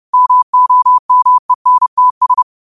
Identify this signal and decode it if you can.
Morse code, heard as a keyed tone, about 30 wpm, 1000 Hz. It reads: MOMENTS
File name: qotdfiles / 30wpm